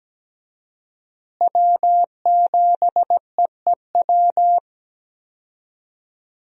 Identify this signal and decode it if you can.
Morse code, heard as a keyed tone, about 17 wpm, 700 Hz. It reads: W7EEW
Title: USA Call Signs - 5 Characters - Repeated 2x - 17wpm